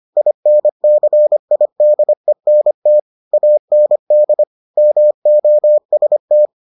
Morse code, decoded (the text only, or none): INCIDENT AND MOST